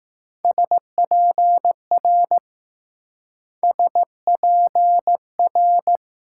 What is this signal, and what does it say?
Morse code, heard as a keyed tone, about 19 wpm, 700 Hz. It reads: SPR SPR